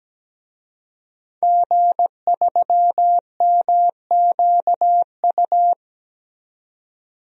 Morse code, heard as a keyed tone, about 17 wpm, 700 Hz. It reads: G3MQU
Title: DXCC-100 Call Signs - 5 Characters - 17wpm